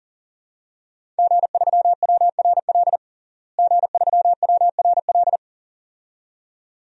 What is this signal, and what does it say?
Morse code, heard as a keyed tone, about 40 wpm, 700 Hz. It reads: G3WRL G3WRL
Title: DXCC-100 Call Signs - 5 Characters - Repeated 2x - 40wpm